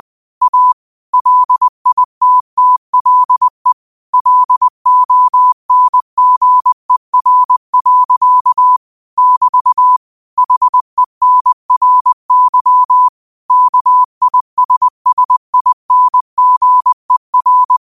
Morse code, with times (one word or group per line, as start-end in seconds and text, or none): 0.4-0.7 A
1.1-3.7 LITTLE
4.1-8.8 LONGER.
9.2-10.0 <BT>
10.4-13.1 HENRY
13.5-17.8 KISSINGER